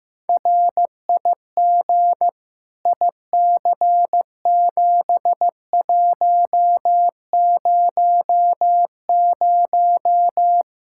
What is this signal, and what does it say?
Morse code, heard as a keyed tone, about 15 wpm, 700 Hz. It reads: RIG IC7100
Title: K1USN SST QSO Practice 15wpm